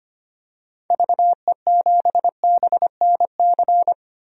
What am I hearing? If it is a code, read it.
Morse code, heard as a keyed tone, about 25 wpm, 700 Hz. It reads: VE7BNC
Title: DXCC-100 Call Signs - 6 Characters - Repeated 2x - 25wpm